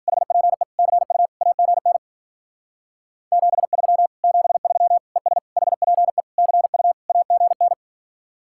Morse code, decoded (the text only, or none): HPE CU AGN 73 73 ES HPE CU AGN